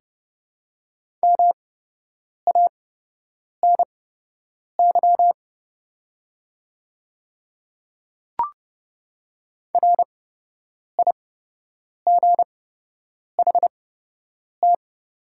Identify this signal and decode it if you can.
Morse code, heard as a keyed tone, about 30 wpm, spaced as for 8 wpm, 700 Hz. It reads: MANY RIGHT
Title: 8x Character Spacing - Top 200 Words - 30wpm